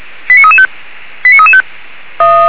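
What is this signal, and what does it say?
Five-tone selective call (ZVEI), heard as quick runs of five tones, tones of about 70 ms. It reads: 89335 89335
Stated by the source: ZVEI five-tone selective call 89330 received via SDR by feuerwehr-orlamuende.org (Feuerwehr Orlamünde fire brigade).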